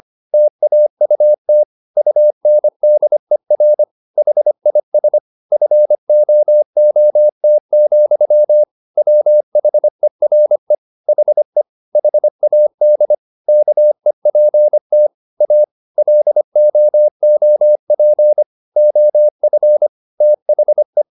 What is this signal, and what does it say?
Morse code, heard as a keyed tone, about 25 wpm, 600 Hz. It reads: TAUT UNDER HIS FOOT, WHERE HE HAD KEPT A LOOP OF THE
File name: the old man and the sea part 1 25wpm